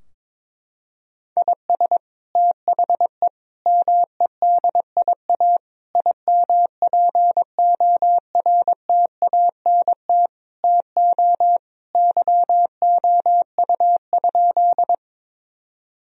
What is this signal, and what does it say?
Morse code, heard as a keyed tone, about 22 wpm, 700 Hz. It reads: IS THE MEDIA IMPORTANT TO YOU?
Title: Sentences from Top 1000 Words - Review Part 4 - 22wpm